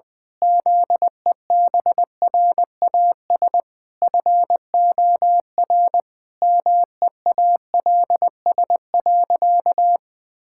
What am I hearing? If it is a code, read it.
Morse code, heard as a keyed tone, about 20 wpm, 700 Hz. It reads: ZEBRAS FOR MEALS.
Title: Pangram Challenge v2 - 20wpm